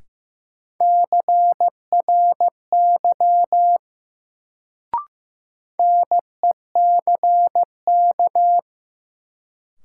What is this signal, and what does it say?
Morse code, heard as a keyed tone, about 15 wpm, 700 Hz. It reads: CRY NECK
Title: Sets of 5 Words Encore - New Word Drill - 15wpm